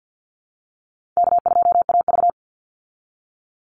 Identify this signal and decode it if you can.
Morse code, heard as a keyed tone, about 48 wpm, 700 Hz. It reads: K2AV
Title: Easy Call Signs Repeated 3x - 50wpm